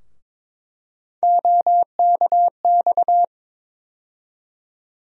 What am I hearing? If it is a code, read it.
Morse code, heard as a keyed tone, about 22 wpm, 700 Hz. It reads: OKX
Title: USA Call Sign Postfixes - 3 Letters 22wpm